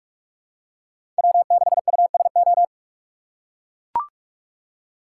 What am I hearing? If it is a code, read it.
Morse code, heard as a keyed tone, about 44 wpm, 700 Hz. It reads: W6USO
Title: USA Call Signs - 5 Characters - 6x Character Spacing - 45wpm